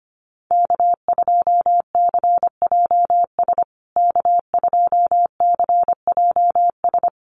Morse code, heard as a keyed tone, about 25 wpm, 700 Hz. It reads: K2CJS K2CJS
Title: USA Call Signs Repeated 3x 25wpm